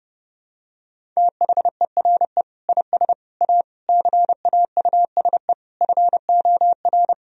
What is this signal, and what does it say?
Morse code, heard as a keyed tone, about 30 wpm, 700 Hz. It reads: THERE IS A CAUSE FOR
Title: Sentences from Top 1000 Words - Review Part 1 - 30wpm